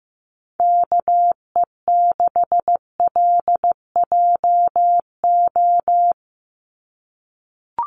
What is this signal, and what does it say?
Morse code, heard as a keyed tone, about 15 wpm, 700 Hz. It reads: KE6LJO
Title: USA Call Signs 15wpm